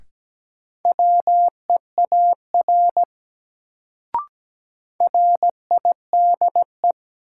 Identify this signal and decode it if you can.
Morse code, heard as a keyed tone, about 17 wpm, 700 Hz. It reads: WEAR RIDE
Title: Sentences from Top 1000 Words - Review Part 6 - New Word Drill - 17wpm